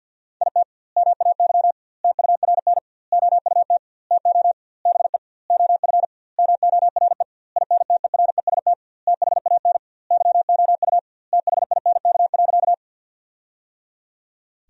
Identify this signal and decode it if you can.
Morse code, heard as a keyed tone, about 50 wpm, 700 Hz. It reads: IT MAY TURN OUT TO BE OF MORE INTEREST THAN YOU THINK.
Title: 2x Word-Spacing - Sentences from Top 400 Words - 50wpm